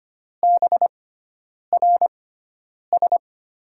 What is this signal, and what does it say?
Morse code, heard as a keyed tone, about 25 wpm, spaced as for 9 wpm, 700 Hz. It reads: BRS
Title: USA Call Signs - 6x Character Spacing - 25wpm